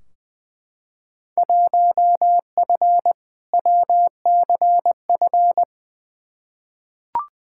Morse code, heard as a keyed tone, about 20 wpm, 700 Hz. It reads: 1F WCF
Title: Field Day Exchanges - 20wpm